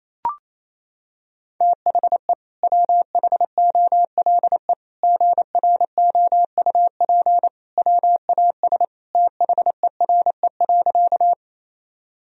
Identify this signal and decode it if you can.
Morse code, heard as a keyed tone, about 28 wpm, 700 Hz. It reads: THE WHOLE GROUP WAS THERE.
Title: Sentences from Top 1000 Words - Review Part 1 - 28wpm